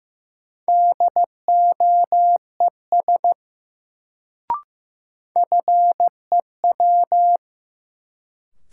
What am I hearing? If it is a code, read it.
Morse code, heard as a keyed tone, about 15 wpm, 700 Hz. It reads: DOES FEW
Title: Sets of 2 Words - New Word Drill - 15wpm